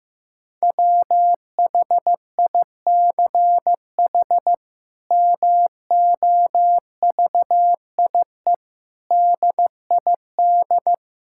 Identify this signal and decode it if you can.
Morse code, heard as a keyed tone, about 16 wpm, 700 Hz. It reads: WHICH MOVIE DID
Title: Sentences from Top 1000 Words - Review Part 5 - 15wpm